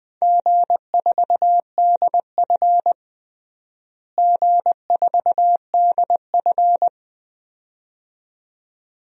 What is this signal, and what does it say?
Morse code, heard as a keyed tone, about 20 wpm, 700 Hz. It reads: G4DF G4DF
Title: DXCC-100 Call Signs - 4 Characters - Repeated 2x - 20wpm